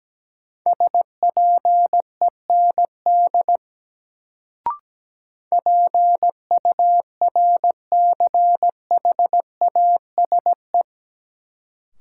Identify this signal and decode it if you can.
Morse code, heard as a keyed tone, about 17 wpm, 700 Hz. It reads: SPEND PURCHASE
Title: Sentences from Top 800 Words - New Word Drill - 17wpm